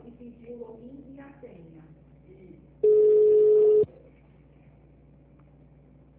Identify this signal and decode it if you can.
Ringback tone (Europe, 425 Hz), with speech.
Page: Bauru - SP